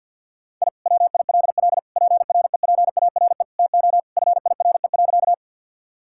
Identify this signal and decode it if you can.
Morse code, heard as a keyed tone, about 52 wpm, 700 Hz. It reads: I WILL PREPARE TO FIRE.
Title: Sentences from Top 1000 Words - Review Part 6 - 50wpm